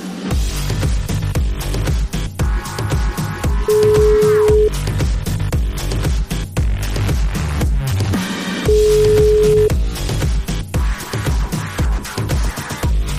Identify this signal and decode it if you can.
Ringback tone (Europe, 425 Hz).